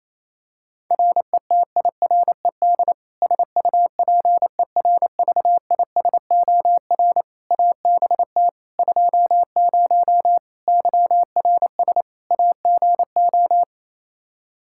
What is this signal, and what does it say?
Morse code, heard as a keyed tone, about 28 wpm, 700 Hz. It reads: RETIRED SUPERVISOR ABT 20 YRS AGO
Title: Practice QSO - Exchange 3 - More Info - 28wpm